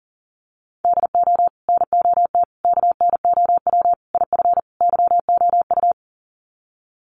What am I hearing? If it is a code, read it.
Morse code, heard as a keyed tone, about 40 wpm, 700 Hz. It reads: DO NOT KNOW IF YOU
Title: Sets of 5 Words 40wpm